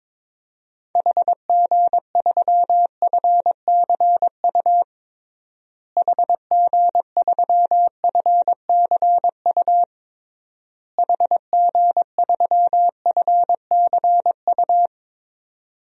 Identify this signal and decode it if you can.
Morse code, heard as a keyed tone, about 21 wpm, 700 Hz. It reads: HG3FCU HG3FCU HG3FCU
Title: DXCC-100 Call Signs - 6 Characters - Repeated 3x - 22wpm